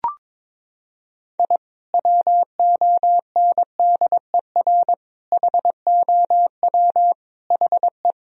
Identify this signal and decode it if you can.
Morse code, heard as a keyed tone, about 23 wpm, 700 Hz. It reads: I WONDER HOW HE
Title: Sentences from Top 1000 Words - Review Part 3 - 22wpm